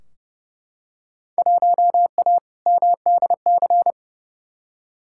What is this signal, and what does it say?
Morse code, heard as a keyed tone, about 30 wpm, 700 Hz. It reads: 1A MDC
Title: Field Day Exchanges - 30wpm